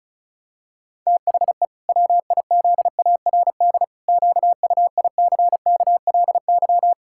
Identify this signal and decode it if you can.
Morse code, heard as a keyed tone, about 34 wpm, 700 Hz. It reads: THE WIZARD QUICKLY
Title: Pangram Challenge v2 - 35wpm